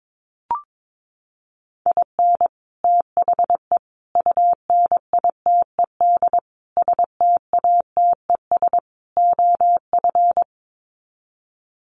Morse code, heard as a keyed tone, about 22 wpm, 700 Hz. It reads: IN THE UNITED STATES OF
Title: Sets of 5 Words Encore2 22wpm